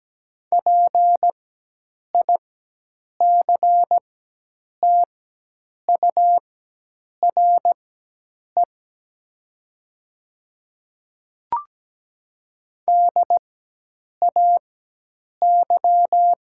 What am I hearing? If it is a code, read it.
Morse code, heard as a keyed tone, about 17 wpm, spaced as for 8 wpm, 700 Hz. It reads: PICTURE DAY
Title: 4x Character Spacing - Top 200 Words - 17wpm